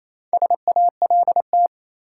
Morse code, heard as a keyed tone, about 28 wpm, 700 Hz. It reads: SALT
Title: Sets of 5 Words - New Word Drill - 28wpm